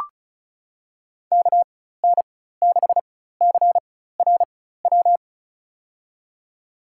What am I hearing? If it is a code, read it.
Morse code, heard as a keyed tone, about 34 wpm, spaced as for 16 wpm, 700 Hz. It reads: KN6CRW
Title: USA Call Signs - 6 Characters - 4x Character Spacing - 35wpm